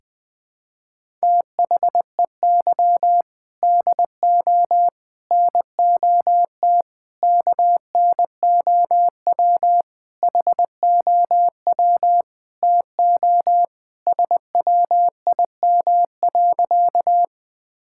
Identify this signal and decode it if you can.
Morse code, heard as a keyed tone, about 20 wpm, 700 Hz. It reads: THEY DO NOT KNOW HOW TO SWIM.
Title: Sentences with 5-Word Phrases - 20wpm